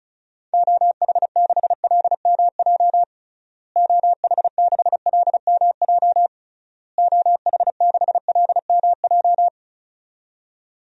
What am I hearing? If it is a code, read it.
Morse code, heard as a keyed tone, about 34 wpm, 700 Hz. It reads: OH6LMJ OH6LMJ OH6LMJ
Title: DXCC-100 Call Signs - 6 Characters - Repeated 3x - 35wpm